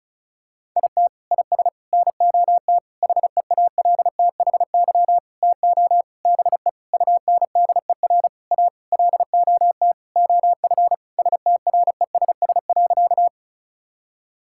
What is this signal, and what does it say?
Morse code, heard as a keyed tone, about 36 wpm, 700 Hz. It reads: IT IS NOT HEALTHY TO BE UNDER A LOT OF STRESS.
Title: Sentences from Top 1000 Words - Review Part 6 - 35wpm